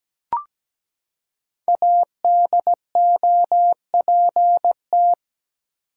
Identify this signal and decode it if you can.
Morse code, heard as a keyed tone, about 17 wpm, 700 Hz. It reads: ADOPT
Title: Five Letter Words - 17wpm